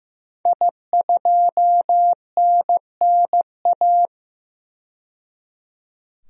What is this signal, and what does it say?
Morse code, heard as a keyed tone, about 15 wpm, 700 Hz. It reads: I2NNA
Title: DXCC-100 Call Signs - 5 Characters - 15wpm